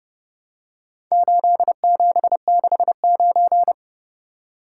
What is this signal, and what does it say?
Morse code, heard as a keyed tone, about 29 wpm, 700 Hz. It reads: 8769